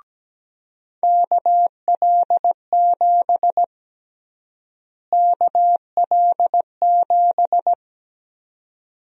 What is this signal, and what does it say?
Morse code, heard as a keyed tone, about 17 wpm, 700 Hz. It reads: KL7 KL7
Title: USA Call Sign Prefixes - Repeated 2x - 17wpm